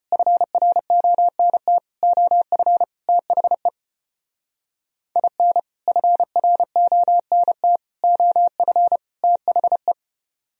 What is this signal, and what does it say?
Morse code, heard as a keyed tone, about 32 wpm, 700 Hz. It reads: FRONT OF THE IN FRONT OF THE